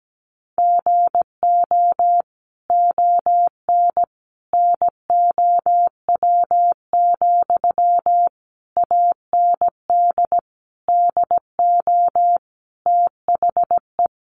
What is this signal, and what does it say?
Morse code, heard as a keyed tone, about 17 wpm, 700 Hz. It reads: GO ON NOW, AND DO THE
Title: Sentences from Top 1500 Words 17wpm